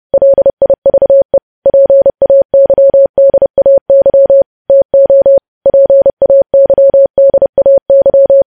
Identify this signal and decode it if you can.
Morse code, heard as a keyed tone, about 30 wpm, 550 Hz. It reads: LIVE PAYDAY TO PAYDAY
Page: news headlines in morse code at 30 wpm